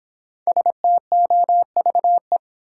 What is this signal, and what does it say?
Morse code, heard as a keyed tone, about 26 wpm, 700 Hz. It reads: STOVE